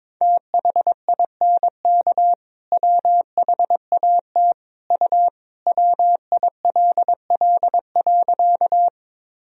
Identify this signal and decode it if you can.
Morse code, heard as a keyed tone, about 22 wpm, 700 Hz. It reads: THINK WHAT U WILL.
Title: Sentences from Top 200 Words and Abbreviations 22wpm